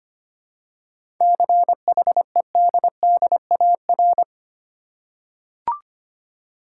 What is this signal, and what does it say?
Morse code, heard as a keyed tone, about 25 wpm, 700 Hz. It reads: CHEDDAR
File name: practice / 25wpm / Easy Words with Hints - 25wpm